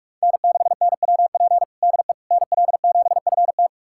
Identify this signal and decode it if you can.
Morse code, heard as a keyed tone, about 42 wpm, 700 Hz. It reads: N6NWP DE NL7FT
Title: USA Callsign DE Callsign - 5 Characters - 45wpm